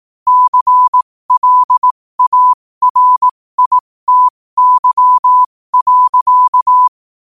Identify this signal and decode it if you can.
Morse code, heard as a keyed tone, about 18 wpm, spaced as for 15 wpm, 1000 Hz. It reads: CLARITY.
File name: qotdfiles / 15wpm